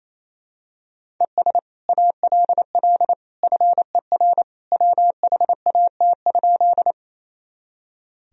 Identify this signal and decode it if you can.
Morse code, heard as a keyed tone, about 28 wpm, 700 Hz. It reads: ES ALL FER WHAT?
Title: Sentences from Top 100 Words and Abbreviations 28wpm